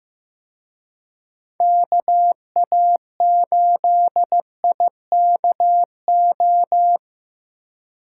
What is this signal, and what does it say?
Morse code, heard as a keyed tone, about 18 wpm, 700 Hz. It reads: KA8IKO
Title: Speed Racing - USA 2x3 Call Signs 15wpm